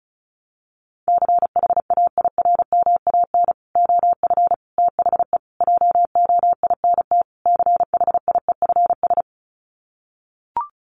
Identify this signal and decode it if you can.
Morse code, heard as a keyed tone, about 34 wpm, 700 Hz. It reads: CHAIRMAN OF THE JOINT CHIEFS
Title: Sets of 5 Words Encore 35wpm